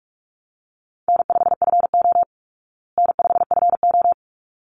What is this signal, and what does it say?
Morse code, heard as a keyed tone, about 44 wpm, 700 Hz. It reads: N5FO N5FO
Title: Call Signs Repeated 3x - 45wpm